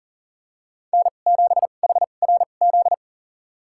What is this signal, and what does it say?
Morse code, heard as a keyed tone, about 40 wpm, spaced as for 29 wpm, 700 Hz. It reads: N7HRZ
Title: USA Call Signs - 5 Characters - 2x Character Spacing - 40wpm